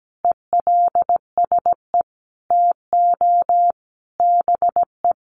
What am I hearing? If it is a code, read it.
Morse code, heard as a keyed tone, about 17 wpm, 700 Hz. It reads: ELSE TO BE
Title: Sentences from Top 800 Words 17wpm